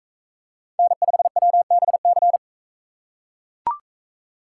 Morse code, heard as a keyed tone, about 38 wpm, 700 Hz. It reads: N5WBC